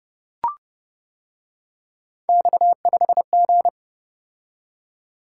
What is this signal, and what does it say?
Morse code, heard as a keyed tone, about 29 wpm, 700 Hz. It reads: X5G